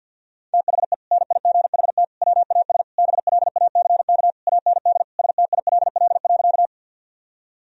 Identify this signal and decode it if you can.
Morse code, heard as a keyed tone, about 52 wpm, 700 Hz. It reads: THE NIGHT WAS BLACK AND STILL.
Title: Sentences from Top 1000 Words - Review Part 2 - 50wpm